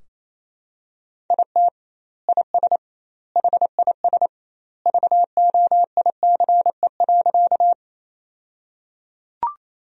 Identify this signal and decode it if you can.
Morse code, heard as a keyed tone, about 28 wpm, 700 Hz. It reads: IT IS HIS VOICE.
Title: 2x Word-Spacing - Sentences from Top 500 Words - 28wpm